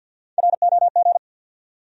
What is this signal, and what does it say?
Morse code, heard as a keyed tone, about 53 wpm, 700 Hz. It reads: ROG
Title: USA Common Names - Max 3 Letters - 50wpm